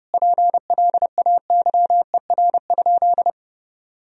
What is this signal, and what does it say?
Morse code, heard as a keyed tone, about 30 wpm, 700 Hz. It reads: PLAYER?
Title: Sentences from Top 1000 Words - Review Part 4 - 30wpm